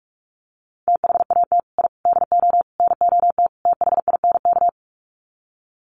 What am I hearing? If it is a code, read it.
Morse code, heard as a keyed tone, about 44 wpm, 700 Hz. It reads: THAT I DO NOT THINK
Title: Sets of 5 Words Encore 45wpm